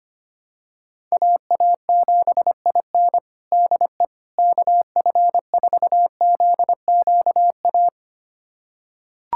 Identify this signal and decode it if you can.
Morse code, heard as a keyed tone, about 25 wpm, 700 Hz. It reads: AA7IN DE KF4ZQA
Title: USA Callsign DE Callsign 25wpm